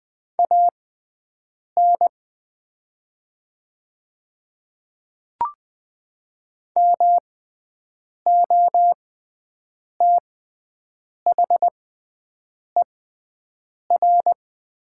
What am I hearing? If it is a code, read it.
Morse code, heard as a keyed tone, about 20 wpm, spaced as for 7 wpm, 700 Hz. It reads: AN MOTHER
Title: 6x Character Spacing - Top 200 Words - 20wpm